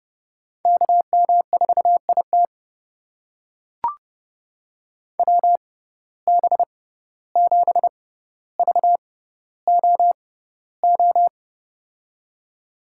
Morse code, heard as a keyed tone, about 32 wpm, spaced as for 15 wpm, 700 Hz. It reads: KM4IT WB7VOO
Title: USA Call Signs - 6x Character Spacing - 30wpm